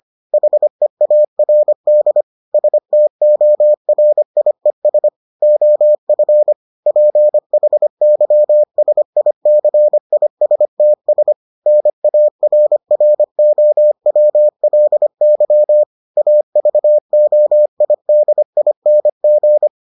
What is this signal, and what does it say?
Morse code, heard as a keyed tone, about 25 wpm, 600 Hz. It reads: HEARD STORIES OF PHYSICISTS NARROWLY AVOIDING